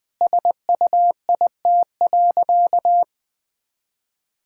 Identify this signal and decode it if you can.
Morse code, heard as a keyed tone, about 20 wpm, 700 Hz. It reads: SUIT.
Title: Sentences from Top 1600 Words 20wpm